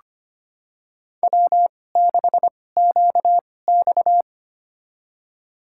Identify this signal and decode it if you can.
Morse code, heard as a keyed tone, about 25 wpm, spaced as for 18 wpm, 700 Hz. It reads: W6QX
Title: USA Call Signs - 4 Characters - 2x Character Spacing - 25wpm